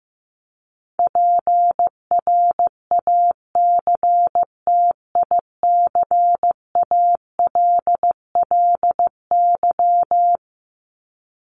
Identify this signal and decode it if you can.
Morse code, heard as a keyed tone, about 15 wpm, 700 Hz. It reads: PRACTICALLY
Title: LY Suffix - Top 100 Words with Base Words 15wpm